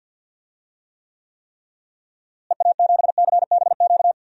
Morse code, heard as a keyed tone, about 50 wpm, 700 Hz. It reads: EA7CBQ
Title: DXCC-100 Call Signs - 6 Characters - Repeated 3x - 50wpm